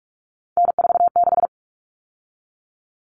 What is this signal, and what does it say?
Morse code, heard as a keyed tone, about 44 wpm, 700 Hz. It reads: N46
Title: Common Cut Numbers - 3-Digit 45wpm